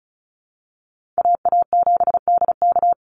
Morse code, heard as a keyed tone, about 34 wpm, 700 Hz. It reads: AA7DK
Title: NA SOTA Chaser Callsigns 35wpm